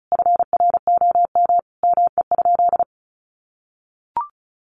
Morse code, heard as a keyed tone, about 34 wpm, 700 Hz. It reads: FROM ME?
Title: Sentences from Top 1500 Words 35wpm